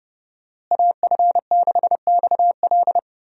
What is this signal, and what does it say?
Morse code, heard as a keyed tone, about 30 wpm, 700 Hz. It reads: AF6XL
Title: USA Call Signs - 5 Characters - Repeated 3x - 30wpm